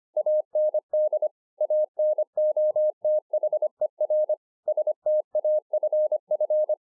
Morse code, heard as a keyed tone, about 26 wpm, 610 Hz. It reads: AND ANOTHER STAFF
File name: 2011-04-mp3 / THE 3.11 3 25WPM